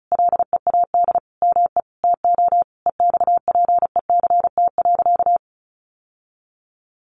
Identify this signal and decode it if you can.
Morse code, heard as a keyed tone, about 34 wpm, 700 Hz. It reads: LEAD ME TO EXPECT.
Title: Sentences from Top 1200 Words 35wpm